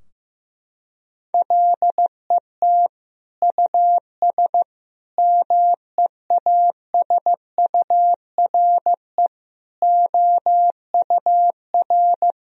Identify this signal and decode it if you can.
Morse code, heard as a keyed tone, about 16 wpm, 700 Hz. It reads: LET US MEASURE OUR
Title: Sentences from Top 1000 Words - Review Part 2 - 15wpm